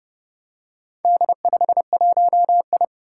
Morse code, heard as a keyed tone, about 30 wpm, 700 Hz. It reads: D51I
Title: Four Challenging Letter-Number Combinations 30wpm